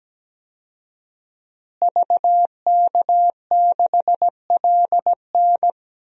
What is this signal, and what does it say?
Morse code, heard as a keyed tone, about 17 wpm, 700 Hz. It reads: VK6LN
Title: DXCC-100 Call Signs - 5 Characters - Repeated 3x - 17wpm